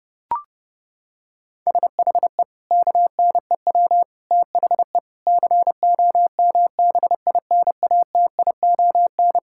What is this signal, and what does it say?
Morse code, heard as a keyed tone, about 32 wpm, 700 Hz. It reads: SHE KNEW THE COMBINATION
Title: Sentences from Top 1000 Words - Review Part 6 - 30wpm